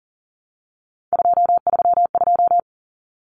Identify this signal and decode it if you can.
Morse code, heard as a keyed tone, about 40 wpm, 700 Hz. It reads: 232
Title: Common Cut Numbers - 3-Digit 40wpm